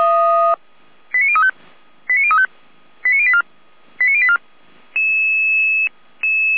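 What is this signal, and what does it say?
Five-tone selective call (ZVEI), heard as quick runs of five tones, tones of about 70 ms. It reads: 89035 89035 89084 89084
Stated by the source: ZVEI five-tone call 89092 received with an SDR.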